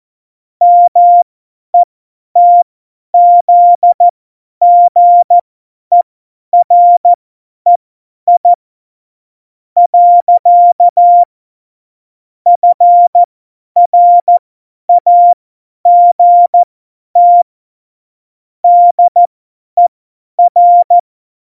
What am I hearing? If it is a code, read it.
Morse code, heard as a keyed tone, about 14 wpm, spaced as for 10 wpm, 700 Hz. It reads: METZGEREI. FRAGT DER
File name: Morsekurs / Witze / Witz 09. Oktober 2018, 50 F70